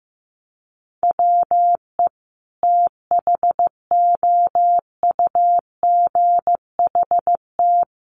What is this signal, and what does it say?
Morse code, heard as a keyed tone, about 14 wpm, 700 Hz. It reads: WE THOUGHT
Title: Sentences from Top 800 Words 15wpm